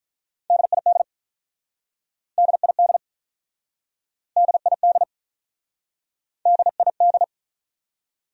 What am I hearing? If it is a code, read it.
Morse code, heard as a keyed tone, about 43 wpm, 700 Hz. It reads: DID DID DID DID
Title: Speed-Racing Top 100 Words 35wpm